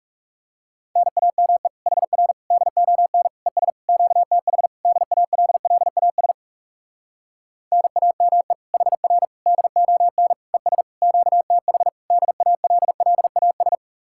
Morse code, heard as a keyed tone, about 40 wpm, 700 Hz. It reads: NAME HR DON ES QTH DALLAS NAME HR DON ES QTH DALLAS